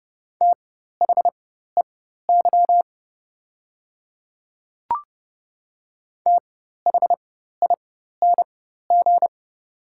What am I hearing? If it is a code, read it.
Morse code, heard as a keyed tone, about 30 wpm, spaced as for 14 wpm, 700 Hz. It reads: THEY THING
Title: 4x Character Spacing - Top 100 Words - 30wpm